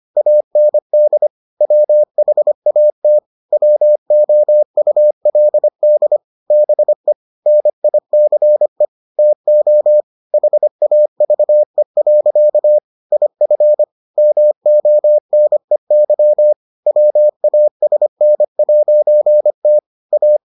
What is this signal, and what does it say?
Morse code, heard as a keyed tone, about 25 wpm, 600 Hz. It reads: AND WHAT WOULD BE NICE TO HAVE. IF MONEY WASN'T A